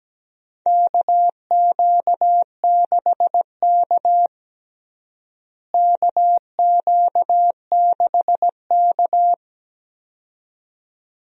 Morse code, heard as a keyed tone, about 17 wpm, 700 Hz. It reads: KQ6K KQ6K
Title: USA Call Signs - 4 Characters - Repeated 2x - 17wpm